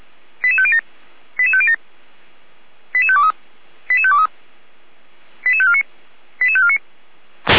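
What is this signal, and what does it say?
Five-tone selective call (ZVEI), heard as quick runs of five tones, tones of about 70 ms. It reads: 88507 88507 88523 88523 88549 88549
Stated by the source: ZVEI five-tone selective call received on an SDR.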